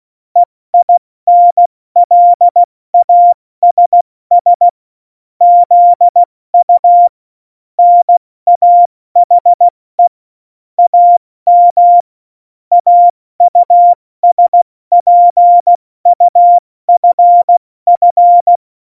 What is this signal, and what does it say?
Morse code, heard as a keyed tone, about 16 wpm, 700 Hz. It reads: EINLASS ZU NAHE AM AUSPUFF
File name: Morsekurs / Witze / Witz 26. Januar 2021, 70 F80